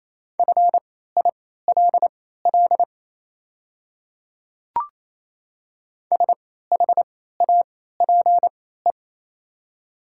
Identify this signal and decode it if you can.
Morse code, heard as a keyed tone, about 28 wpm, spaced as for 16 wpm, 690 Hz. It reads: FILL SHAPE
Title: 3x Character Spacing - Top 500 Words - 28wpm